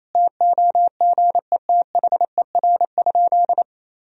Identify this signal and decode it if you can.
Morse code, heard as a keyed tone, about 28 wpm, 700 Hz. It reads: TOGETHER?
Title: Sentences from Top 1000 Words - Review Part 4 - 28wpm